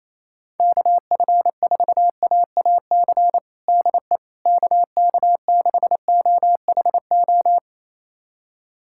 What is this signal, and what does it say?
Morse code, heard as a keyed tone, about 28 wpm, 700 Hz. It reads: KF4AAC DE KK6OHO
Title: USA Callsign DE Callsign 28wpm